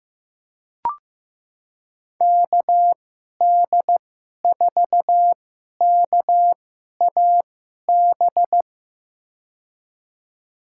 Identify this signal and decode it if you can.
Morse code, heard as a keyed tone, about 16 wpm, spaced as for 11 wpm, 700 Hz. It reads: KD4KAB
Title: USA Call Signs - 6 Characters - 2x Character Spacing - 15wpm